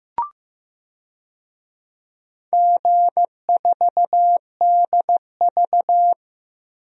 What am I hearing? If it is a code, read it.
Morse code, heard as a keyed tone, about 15 wpm, 700 Hz. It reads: G4DV
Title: DXCC-100 Call Signs - 4 Characters - 15wpm